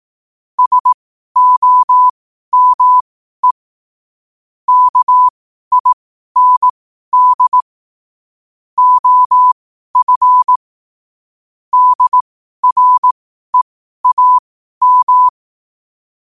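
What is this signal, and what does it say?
Morse code, heard as a keyed tone, about 18 wpm, spaced as for 12 wpm, 1000 Hz. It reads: SOME KIND OF DREAM
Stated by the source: Morse code sent at 12 words per minute.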